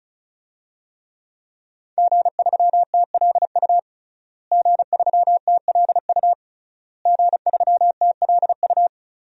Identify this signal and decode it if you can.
Morse code, heard as a keyed tone, about 36 wpm, 700 Hz. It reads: G3TLU G3TLU G3TLU
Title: DXCC-100 Call Signs - 5 Characters - Repeated 3x - 35wpm